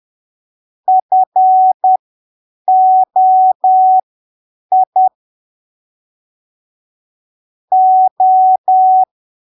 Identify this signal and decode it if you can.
Morse code, heard as a keyed tone, about 10 wpm, spaced as for 7 wpm, 750 Hz. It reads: FOI O
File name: arquivos / texto / texto 16 10ppm